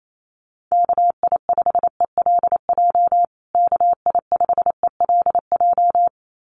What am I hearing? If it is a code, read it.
Morse code, heard as a keyed tone, about 28 wpm, 700 Hz. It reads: KI5ELJ KI5ELJ
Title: USA Call Signs Repeated 2x 28wpm